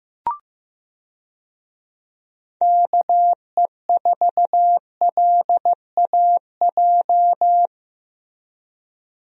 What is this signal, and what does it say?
Morse code, heard as a keyed tone, about 15 wpm, 700 Hz. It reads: KE4LAJ
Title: USA Call Signs - 6 Characters - Repeated 3x - 15wpm